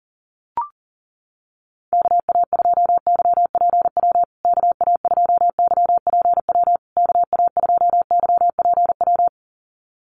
Morse code, heard as a keyed tone, about 40 wpm, 700 Hz. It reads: KA2YPW KA2YPW KA2YPW
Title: USA Call Signs Repeated 3x 40wpm